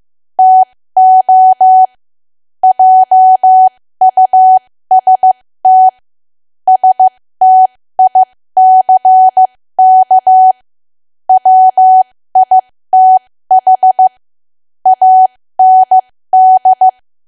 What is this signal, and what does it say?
Morse code, heard as a keyed tone, about 15 wpm, spaced as for 13 wpm, 750 Hz. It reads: TO JUST STICK WITH AND
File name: cwa / W1AW-Level2 / 13WPM